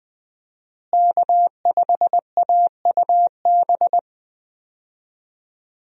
Morse code, heard as a keyed tone, about 20 wpm, 700 Hz. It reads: K5AUB
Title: USA Call Signs - 5 Characters - Repeated 2x - 20wpm